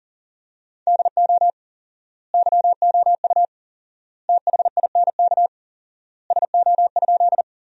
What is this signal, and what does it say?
Morse code, heard as a keyed tone, about 37 wpm, 700 Hz. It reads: DO YOU THINK SO?
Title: 4x Word-Spacing - Sentences from Top 200 Words - 40wpm